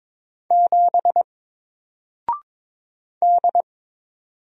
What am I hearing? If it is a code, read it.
Morse code, heard as a keyed tone, about 22 wpm, 700 Hz. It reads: 7 D
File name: Single-Letter-Number-22wpm